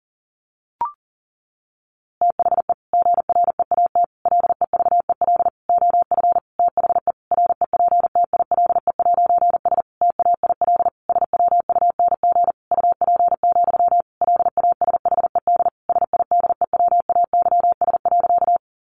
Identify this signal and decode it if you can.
Morse code, heard as a keyed tone, about 40 wpm, 700 Hz. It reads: THE GREAT LEVEL OF THE REPTILE'S TAIL SWUNG UP, LASHED SIDEWAYS.
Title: A Sound Of Thunder 40wpm